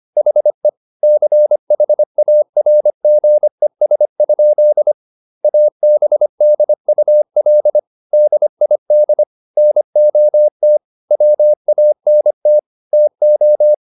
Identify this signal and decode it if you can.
Morse code, heard as a keyed tone, about 25 wpm, 600 Hz. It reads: HE CHARGES? ABDUL DID NOT WANT TO